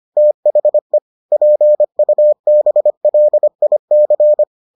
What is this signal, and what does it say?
Morse code, heard as a keyed tone, about 25 wpm, 600 Hz. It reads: THE PUBLIC